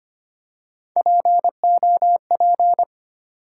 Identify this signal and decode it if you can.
Morse code, heard as a keyed tone, about 25 wpm, 700 Hz. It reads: POP
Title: Three Letter Words - 25wpm